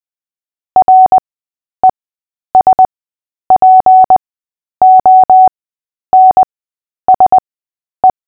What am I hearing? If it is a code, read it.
Morse code, heard as a keyed tone, about 20 wpm, spaced as for 10 wpm, 750 Hz. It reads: RESPONSE
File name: cw / fists / Short-Story-8-10wpm